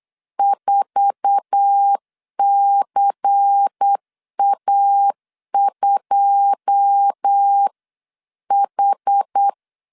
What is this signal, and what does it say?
Morse code, heard as a keyed tone, about 8 wpm, 800 Hz. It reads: 4CA2H
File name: CW / DET / 1 9 1500C S C 8wpm 5653